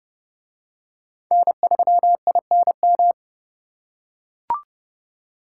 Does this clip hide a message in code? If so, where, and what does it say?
Yes: Morse code from 1.3 to 3.1 s, heard as a keyed tone, about 27 wpm, 700 Hz. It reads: N3INM